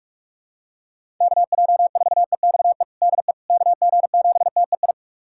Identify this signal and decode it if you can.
Morse code, heard as a keyed tone, about 41 wpm, 700 Hz. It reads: KJ4EXE DE KG7TEI
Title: USA Callsign DE Callsign - 6 Characters - 45wpm